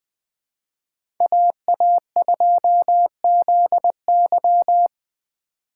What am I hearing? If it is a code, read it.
Morse code, heard as a keyed tone, about 20 wpm, 700 Hz. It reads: AA2ZY
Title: USA Call Signs - 5 Characters - 6x Character Spacing - 20wpm